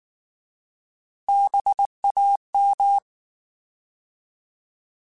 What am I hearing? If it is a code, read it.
Morse code, heard as a keyed tone, about 19 wpm, 800 Hz. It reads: BAM
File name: Alphabetical / Alphabetical mp3s / Bs @18